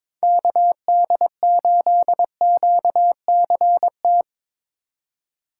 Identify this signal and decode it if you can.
Morse code, heard as a keyed tone, about 22 wpm, 700 Hz. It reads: KD8QCT
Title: USA Call Signs - 4x Character Spacing - 22wpm